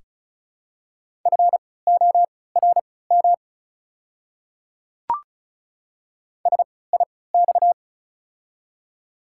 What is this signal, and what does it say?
Morse code, heard as a keyed tone, about 34 wpm, spaced as for 20 wpm, 700 Hz. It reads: FORM SIX
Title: 3x Character Spacing - Top 500 Words - 35wpm